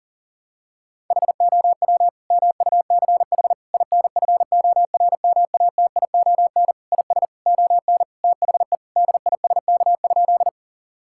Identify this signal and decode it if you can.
Morse code, heard as a keyed tone, about 40 wpm, 700 Hz. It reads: HOW MUCH INFORMATION IS ON THE DISK?
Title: Sentences from Top 1000 Words - Review Part 6 - 40wpm